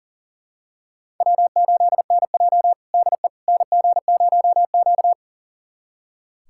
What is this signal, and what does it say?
Morse code, heard as a keyed tone, about 40 wpm, 700 Hz. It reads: W8NJ DE NG0Q
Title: USA Callsign DE Callsign - 4 Characters - 40wpm